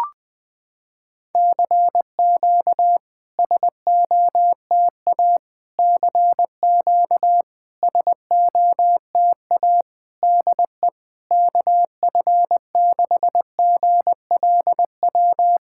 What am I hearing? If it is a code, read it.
Morse code, heard as a keyed tone, about 20 wpm, 700 Hz. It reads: CQ SOTA CQ SOTA DE KF6GLW
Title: SOTA QSO Practice 20wpm